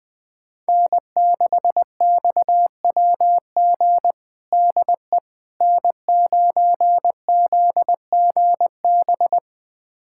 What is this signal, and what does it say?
Morse code, heard as a keyed tone, about 20 wpm, 700 Hz. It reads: N6XWG DE N9ZGB
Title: USA Callsign DE Callsign - 5 Characters - 20wpm